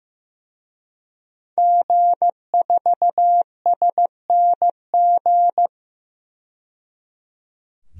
Morse code, heard as a keyed tone, about 15 wpm, 700 Hz. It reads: G4SNG
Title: DXCC-100 Call Signs - 5 Characters - 15wpm